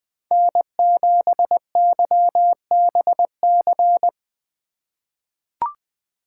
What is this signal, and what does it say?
Morse code, heard as a keyed tone, about 20 wpm, 700 Hz. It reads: N7YBC
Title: USA Call Signs - 5 Characters - 2x Character Spacing - 20wpm